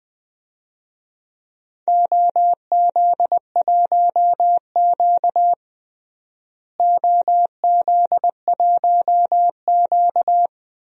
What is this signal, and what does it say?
Morse code, heard as a keyed tone, about 20 wpm, 700 Hz. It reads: OZ1Q OZ1Q
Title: DXCC-100 Call Signs - 4 Characters - Repeated 3x - 20wpm